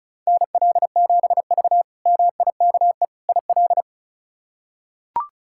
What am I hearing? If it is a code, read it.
Morse code, heard as a keyed tone, about 34 wpm, 700 Hz. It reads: NL7V MIKE IL
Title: K1USN SST QSO Practice - 35wpm